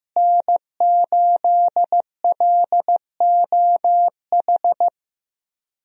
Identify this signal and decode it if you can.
Morse code, heard as a keyed tone, about 15 wpm, 700 Hz. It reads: N8LOH
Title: USA Call Signs - 5 Characters - Repeated 2x - 15wpm